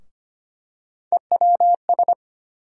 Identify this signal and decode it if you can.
Morse code, heard as a keyed tone, about 25 wpm, 700 Hz. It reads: EWS
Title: USA Call Sign Postfixes - 3 Letters 25wpm